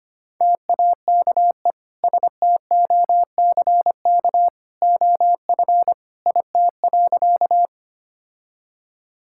Morse code, heard as a keyed tone, about 25 wpm, 700 Hz. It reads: TAKE STOCK OF IT.
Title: Sentences from Top 1000 Words - Review Part 5 - 25wpm